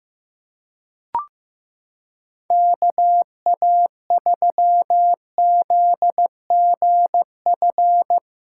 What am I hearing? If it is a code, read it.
Morse code, heard as a keyed tone, about 15 wpm, 700 Hz. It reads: KA3ZGF
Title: USA Callsign DE Callsign 15wpm